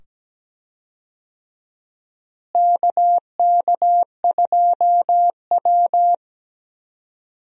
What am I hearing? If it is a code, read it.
Morse code, heard as a keyed tone, about 17 wpm, 700 Hz. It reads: KK2W
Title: USA Call Signs - 4 Characters - Repeated 2x - 17wpm